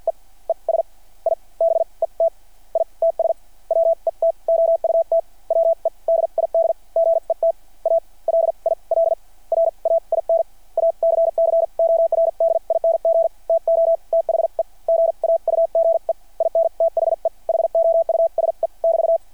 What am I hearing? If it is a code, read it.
Morse code, heard as a keyed tone, about 48 wpm, spaced as for 39 wpm, 600 Hz. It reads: EE S I BET ITS WET OUT WE DID GET A LIL RAIN ACCORDING TO THE GAUGE IN THE HOUSE <BT>
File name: archived / CRSnet